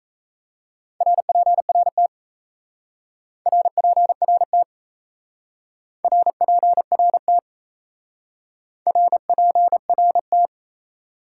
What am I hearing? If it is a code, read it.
Morse code, heard as a keyed tone, about 34 wpm, 700 Hz. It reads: RPRT RPRT RPRT RPRT
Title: Speed-Racing Top 180 QSO Elements 28wpm